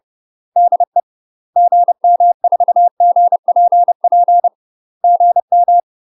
Morse code, heard as a keyed tone, about 30 wpm, 700 Hz. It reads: DE GM4GPP GM
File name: UK MASTER QSOs TRAINING / 30WPM / UK MASTER QSOs 0007